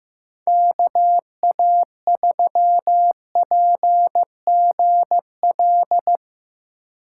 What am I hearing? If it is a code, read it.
Morse code, heard as a keyed tone, about 15 wpm, 700 Hz. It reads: KA3PGL
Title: USA Call Signs - 6 Characters - 4x Character Spacing - 15wpm